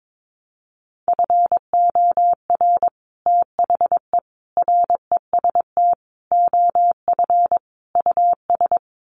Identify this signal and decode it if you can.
Morse code, heard as a keyed tone, about 22 wpm, 700 Hz. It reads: FOR THE REST OF US
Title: Sets of 5 Words Encore 22wpm